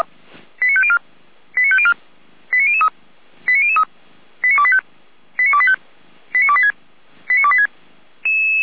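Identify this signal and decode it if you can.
Five-tone selective call (ZVEI), heard as quick runs of five tones, tones of about 70 ms. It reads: 89504 89504 89003 89003 89285 89285 89286 89286
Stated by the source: Received on SDR, ZVEI five-tone selective call.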